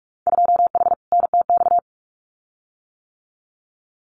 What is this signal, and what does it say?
Morse code, heard as a keyed tone, about 46 wpm, 700 Hz. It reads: 2H NTX
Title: Winter Field Day Exchanges Encore 45wpm